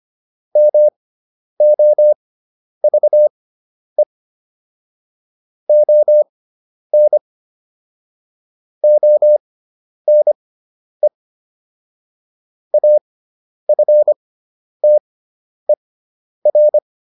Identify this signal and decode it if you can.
Morse code, heard as a keyed tone, about 25 wpm, spaced as for 10 wpm, 600 Hz. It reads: MOVE ON ONE AFTER